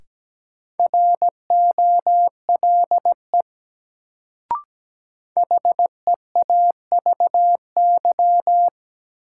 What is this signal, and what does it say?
Morse code, heard as a keyed tone, about 17 wpm, 700 Hz. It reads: ROLE HEAVY
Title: Sentences from Top 1000 Words - Review Part 6 - New Word Drill - 17wpm